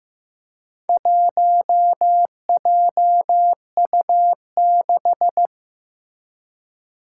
Morse code, heard as a keyed tone, about 15 wpm, 700 Hz. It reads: 1JU6
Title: Four Challenging Letter-Number Combinations 15wpm